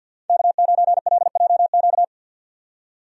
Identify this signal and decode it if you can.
Morse code, heard as a keyed tone, about 50 wpm, 700 Hz. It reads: K9LJQ
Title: USA Call Signs - 5 Characters - 50wpm